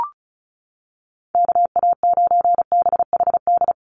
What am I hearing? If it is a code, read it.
Morse code, heard as a keyed tone, about 34 wpm, 700 Hz. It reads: KA9BHD
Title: Call Signs - 35wpm